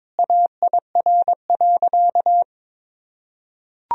Morse code, heard as a keyed tone, about 22 wpm, 700 Hz. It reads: AIR.
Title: Sentences from Top 1000 Words - Review Part 1 - 22wpm